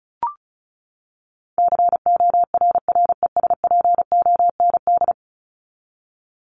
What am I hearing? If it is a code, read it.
Morse code, heard as a keyed tone, about 34 wpm, 700 Hz. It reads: CORRESPOND
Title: ING Suffix - Top 100 Words with Base Words 35wpm